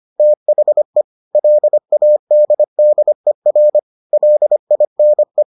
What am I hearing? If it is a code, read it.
Morse code, heard as a keyed tone, about 25 wpm, 600 Hz. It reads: THE LADDER LINE